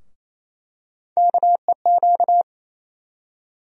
Morse code, heard as a keyed tone, about 28 wpm, 700 Hz. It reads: KEQ
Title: USA Call Sign Postfixes - 3 Letters 28wpm